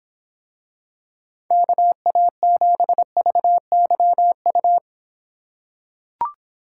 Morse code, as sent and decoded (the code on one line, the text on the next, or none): -.- .- --... ...- -.-- ..-
KA7VYU